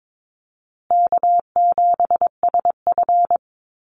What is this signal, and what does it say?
Morse code, heard as a keyed tone, about 22 wpm, 700 Hz. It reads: K7SF
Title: Easy Call Signs Repeated 3x - 22wpm